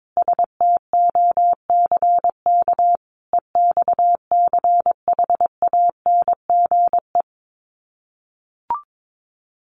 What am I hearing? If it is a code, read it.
Morse code, heard as a keyed tone, about 22 wpm, 700 Hz. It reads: STOCK EXCHANGE
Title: Sets of 5 Words 22wpm